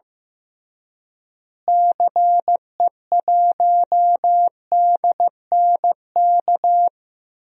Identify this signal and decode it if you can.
Morse code, heard as a keyed tone, about 15 wpm, 700 Hz. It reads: CE1DNK
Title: DXCC-100 Call Signs - 6 Characters - Repeated 2x - 15wpm